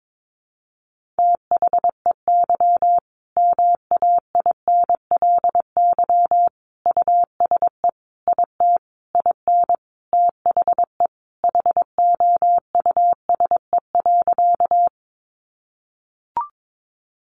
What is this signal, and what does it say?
Morse code, heard as a keyed tone, about 22 wpm, 700 Hz. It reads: THEY MAINLY USE IT IN THE HOUSE.
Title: Sentences from Top 1400 Words 22wpm